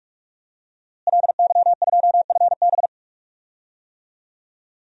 Morse code, heard as a keyed tone, about 46 wpm, 700 Hz. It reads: LY2FB
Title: DXCC-100 Call Signs - 5 Characters - Repeated 2x - 45wpm